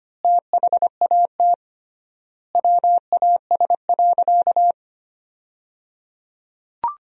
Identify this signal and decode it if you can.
Morse code, heard as a keyed tone, about 25 wpm, 700 Hz. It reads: THAT WAS.
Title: 3x Word-Spacing - Sentences from Top 300 Words - 25wpm